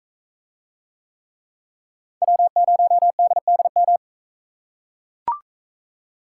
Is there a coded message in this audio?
Yes: Morse code from 2.2 to 4.0 s, heard as a keyed tone, about 42 wpm, 700 Hz. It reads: W0DDM